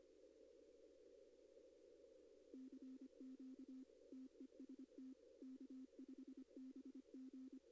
Morse code, heard as a keyed tone, about 25 wpm, 260 Hz. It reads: CQ TEST K5BG
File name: contests / arrl-160-k5tr-op-w5tn -2200